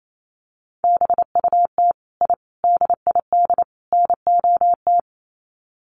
Morse code, heard as a keyed tone, about 28 wpm, 700 Hz. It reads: BUT I DID NOT
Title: Sets of 4 Words 28wpm